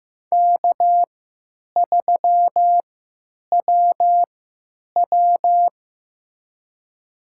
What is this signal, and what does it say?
Morse code, heard as a keyed tone, about 15 wpm, spaced as for 9 wpm, 700 Hz. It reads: K3WW
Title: USA Call Signs - 3x Character Spacing - 15wpm